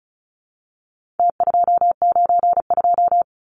Morse code, heard as a keyed tone, about 34 wpm, 700 Hz. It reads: T292
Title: Numbers - Contest Style - 4-Digit 35wpm